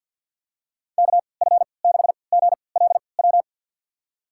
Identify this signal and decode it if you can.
Morse code, heard as a keyed tone, about 51 wpm, spaced as for 28 wpm, 700 Hz. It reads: KF6GLW
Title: USA Call Signs - 6 Characters - 3x Character Spacing - 50wpm